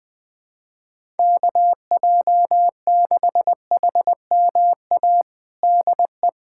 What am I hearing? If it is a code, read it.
Morse code, heard as a keyed tone, about 20 wpm, 700 Hz. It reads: KJ6HMA DE
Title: USA Callsign DE Callsign - 6 Characters - 20wpm